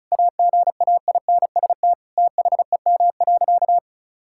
Morse code, heard as a keyed tone, about 36 wpm, 700 Hz. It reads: AGAINST THEM.
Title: Sentences from Top 1600 Words 35wpm